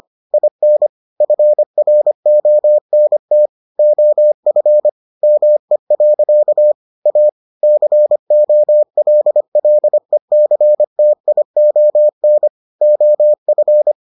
Morse code, heard as a keyed tone, about 25 wpm, 600 Hz. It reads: IN FRONT OF ME. A COLLECTION OF